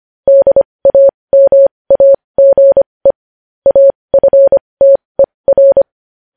Morse code, heard as a keyed tone, about 25 wpm, spaced as for 20 wpm, 550 Hz. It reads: DAMAGE AFTER
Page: News headlines in Morse code at 20 WPM